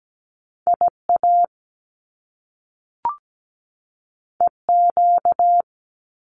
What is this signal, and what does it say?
Morse code, heard as a keyed tone, about 17 wpm, 700 Hz. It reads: IA EQ
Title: Top Two Letter Combinations in Words 17wpm